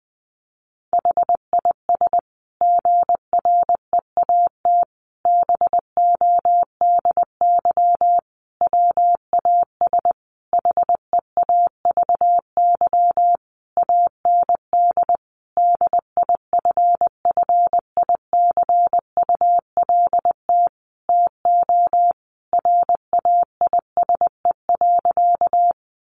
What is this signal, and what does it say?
Morse code, heard as a keyed tone, about 20 wpm, 700 Hz. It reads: HIS GREAT BODY WAS HEAVY AND DIFFICULT TO RAISE.
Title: Sentences from Top 1400 Words 20wpm